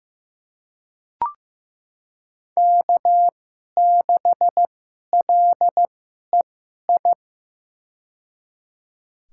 Morse code, heard as a keyed tone, about 15 wpm, spaced as for 11 wpm, 700 Hz. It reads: K6LEI
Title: USA Call Signs - 5 Characters - 2x Character Spacing - 15wpm